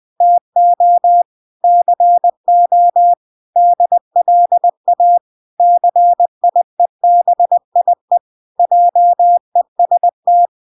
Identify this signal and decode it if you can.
Morse code, heard as a keyed tone, about 20 wpm, 700 Hz. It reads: TO CO DLA CIEBIE JEST